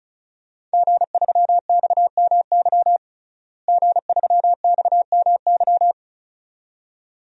Morse code, heard as a keyed tone, about 34 wpm, 700 Hz. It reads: G3XMY G3XMY
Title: DXCC-100 Call Signs - 5 Characters - Repeated 2x - 35wpm